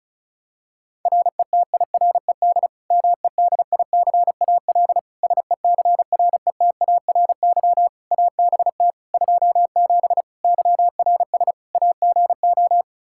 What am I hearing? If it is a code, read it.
Morse code, heard as a keyed tone, about 36 wpm, 700 Hz. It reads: RETIRED MEDICAL SECRETARY ABT 27 YRS AGO
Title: Practice QSO - Exchange 3 - More Info - 35wpm